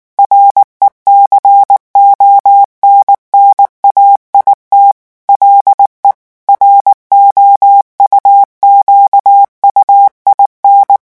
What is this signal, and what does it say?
Morse code, heard as a keyed tone, about 19 wpm, 800 Hz. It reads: RECONNAIT LE ROUQUIN
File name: morse mp3 / revision9 citations humoristiques vitesse 18 mots minute